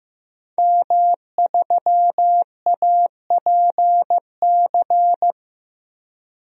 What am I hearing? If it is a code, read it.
Morse code, heard as a keyed tone, about 15 wpm, 700 Hz. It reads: M3APC
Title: DXCC-100 Call Signs - 5 Characters - Repeated 3x - 15wpm